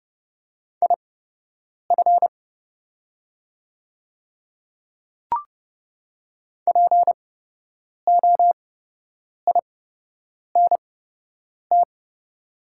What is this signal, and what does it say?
Morse code, heard as a keyed tone, about 30 wpm, spaced as for 8 wpm, 700 Hz. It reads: IF POINT
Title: 8x Character Spacing - Top 200 Words - 30wpm